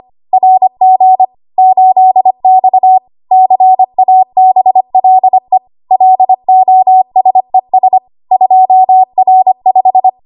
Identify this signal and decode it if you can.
Morse code, heard as a keyed tone, about 25 wpm, 750 Hz. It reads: RG 8X CABLE LOSES 2R5
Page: W1AW Web Code Practice Files